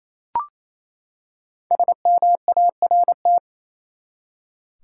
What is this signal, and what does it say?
Morse code, heard as a keyed tone, about 28 wpm, 700 Hz. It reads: SMART
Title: Five Letter Words - 28wpm